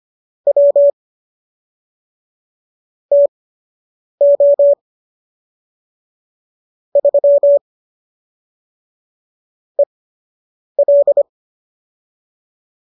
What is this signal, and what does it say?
Morse code, heard as a keyed tone, about 25 wpm, spaced as for 8 wpm, 570 Hz. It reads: W TO 3 EL